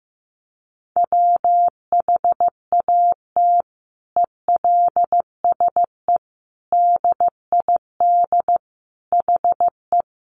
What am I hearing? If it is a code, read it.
Morse code, heard as a keyed tone, about 15 wpm, 700 Hz. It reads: WHAT ELSE DID HE
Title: Sentences from Top 800 Words 15wpm